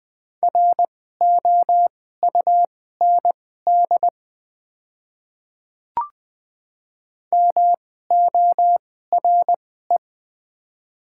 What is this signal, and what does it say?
Morse code, heard as a keyed tone, about 20 wpm, spaced as for 14 wpm, 700 Hz. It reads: ROUND MORE
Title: 2x Character Spacing - Top 200 Words - 20wpm